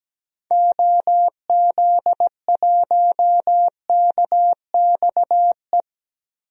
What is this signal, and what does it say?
Morse code, heard as a keyed tone, about 17 wpm, 700 Hz. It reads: OZ1KXE
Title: DXCC-100 Call Signs - 6 Characters - 17wpm